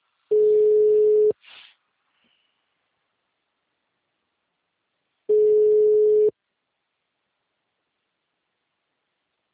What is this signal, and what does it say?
Ringback tone (Europe, 425 Hz).